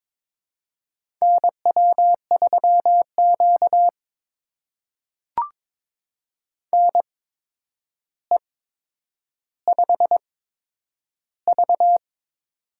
Morse code, heard as a keyed tone, about 22 wpm, spaced as for 9 wpm, 700 Hz. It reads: NW3Q NE5V
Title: USA Call Signs - 4 Characters - 8x Character Spacing - 22wpm